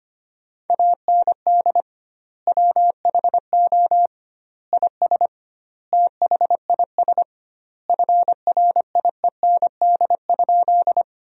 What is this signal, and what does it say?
Morse code, heard as a keyed tone, about 25 wpm, 700 Hz. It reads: AND WHO IS THIS FRIEND?
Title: 2x Word-Spacing - Sentences from Top 300 Words - 25wpm